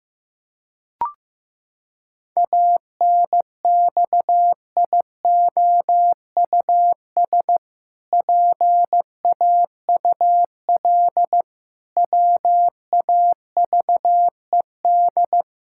Morse code, heard as a keyed tone, about 15 wpm, 700 Hz. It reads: ANXIOUS PAUL WAVED
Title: Pangram Challenge v2 - 15wpm